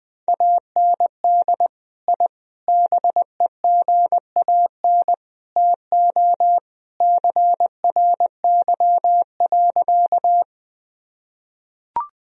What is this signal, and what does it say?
Morse code, heard as a keyed tone, about 20 wpm, 700 Hz. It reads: AND I BEGAN TO CRY.
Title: Sentences from Top 1000 Words - Review Part 2 - 20wpm